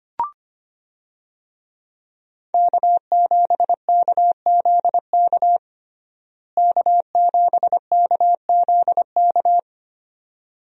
Morse code, heard as a keyed tone, about 25 wpm, 700 Hz. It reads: K7KZK K7KZK
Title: USA Call Signs - 5 Characters - Repeated 2x - 25wpm